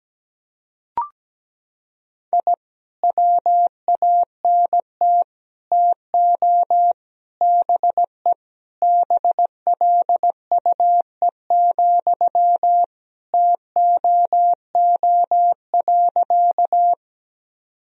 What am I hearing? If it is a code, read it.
Morse code, heard as a keyed tone, about 17 wpm, 700 Hz. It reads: I WANT TO BE BLUE, TOO.
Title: Sentences from Top 1000 Words - Review Part 3 - 17wpm